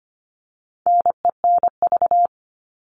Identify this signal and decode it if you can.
Morse code, heard as a keyed tone, about 25 wpm, 700 Hz. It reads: NENV
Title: All Cut Numbers - 4-Digit 25wpm